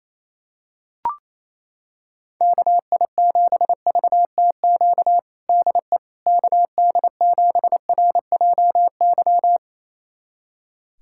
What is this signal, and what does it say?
Morse code, heard as a keyed tone, about 27 wpm, 700 Hz. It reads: KI7VTQ DE KD7RJY
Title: USA Callsign DE Callsign - 6 Characters - 28wpm